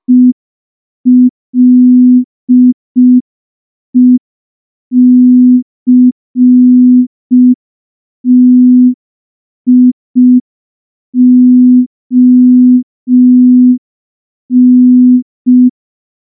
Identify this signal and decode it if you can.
Morse code, heard as a keyed tone, about 5 wpm, 250 Hz. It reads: ELECTION
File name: morse / morse-training-05WPM